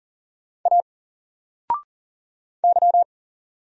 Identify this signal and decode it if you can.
Morse code, heard as a keyed tone, about 40 wpm, 700 Hz. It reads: A Y
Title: Single Letter-Number 40wpm